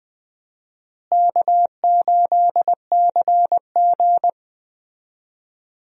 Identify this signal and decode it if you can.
Morse code, heard as a keyed tone, about 20 wpm, 700 Hz. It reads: K8CG
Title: USA Call Signs - 4 Characters - Repeated 3x - 20wpm